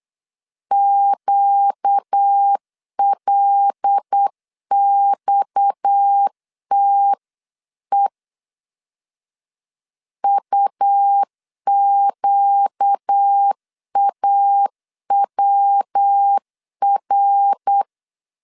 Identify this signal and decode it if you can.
Morse code, heard as a keyed tone, about 8 wpm, 800 Hz. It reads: QLXTE UQAWR
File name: LEZIONI / 6 PKX0 / 6 PKX 8wpm 3041